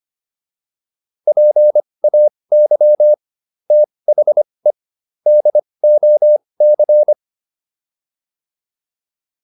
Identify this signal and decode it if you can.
Morse code, heard as a keyed tone, about 25 wpm, spaced as for 20 wpm, 600 Hz. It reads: PAY THE DOC